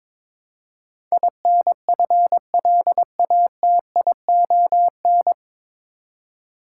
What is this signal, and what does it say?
Morse code, heard as a keyed tone, about 23 wpm, 700 Hz. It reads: INFLATION
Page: Morse Code Ninja practice 22wpm